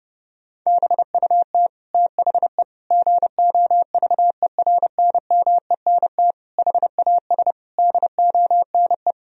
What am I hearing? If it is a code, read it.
Morse code, heard as a keyed tone, about 30 wpm, 700 Hz. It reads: BUT THE GOVERNMENT HAS DONE
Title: Sentences from Top 1000 Words - Review Part 3 - 30wpm